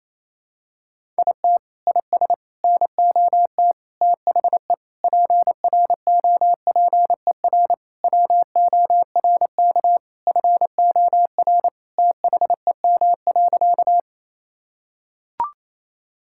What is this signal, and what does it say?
Morse code, heard as a keyed tone, about 28 wpm, 700 Hz. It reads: IT IS NOT THE PROPER WORK FOR THEM.
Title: Sentences from Top 1000 Words - Review Part 5 - 28wpm